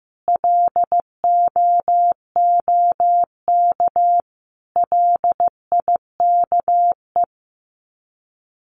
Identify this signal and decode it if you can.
Morse code, heard as a keyed tone, about 15 wpm, 700 Hz. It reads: LOOK LIKE
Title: Sets of 5 Words Encore2 15wpm